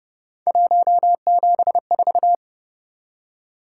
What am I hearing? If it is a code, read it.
Morse code, heard as a keyed tone, about 29 wpm, 700 Hz. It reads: 174